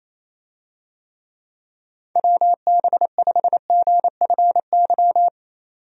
Morse code, heard as a keyed tone, about 27 wpm, 700 Hz. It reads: WB5GFY
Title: USA Call Signs - 6 Characters - 28wpm